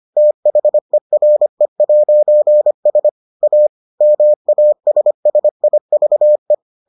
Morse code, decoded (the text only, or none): THERE'S A MASSIVE